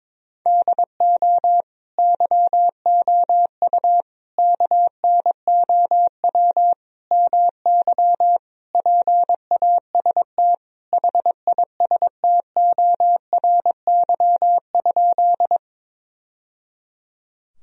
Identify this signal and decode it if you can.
Morse code, heard as a keyed tone, about 22 wpm, 700 Hz. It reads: DO YOU KNOW MY PAST HISTORY?
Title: Sentences from Top 1000 Words - Review Part 3 - 22wpm